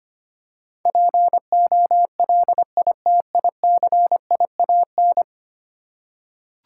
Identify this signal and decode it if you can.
Morse code, heard as a keyed tone, about 25 wpm, 700 Hz. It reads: POLITICIAN
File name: Ten Letter Words - 25wpm